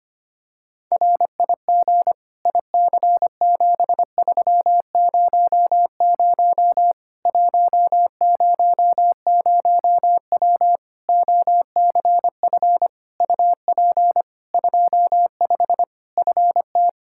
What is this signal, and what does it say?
Morse code, heard as a keyed tone, about 25 wpm, 700 Hz. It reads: RIG IC7300 100W OCF UP 25 FT
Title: Practice QSO - Exchange 2 - Info - 25wpm